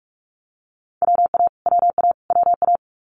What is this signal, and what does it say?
Morse code, heard as a keyed tone, about 46 wpm, 700 Hz. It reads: WA WA WA
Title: Multi-letter Phonemes Repeated 3x 45wpm